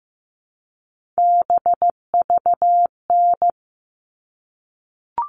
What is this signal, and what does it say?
Morse code, heard as a keyed tone, about 15 wpm, 700 Hz. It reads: BVN
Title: All Cut Numbers - 3-Digit 15wpm